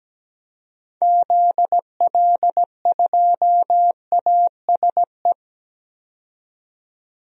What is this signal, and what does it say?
Morse code, heard as a keyed tone, about 17 wpm, 700 Hz. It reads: ZL2ASE
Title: DXCC-100 Call Signs - 6 Characters - 17wpm